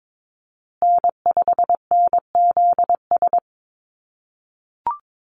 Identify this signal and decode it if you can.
Morse code, heard as a keyed tone, about 22 wpm, 690 Hz. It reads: N5NZS
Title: USA Call Signs Repeated 3x 22wpm